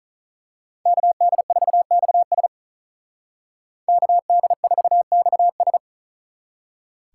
Morse code, heard as a keyed tone, about 39 wpm, 700 Hz. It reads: KD4XS KD4XS
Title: Speed Racing - USA 2x2 Call Signs 35wpm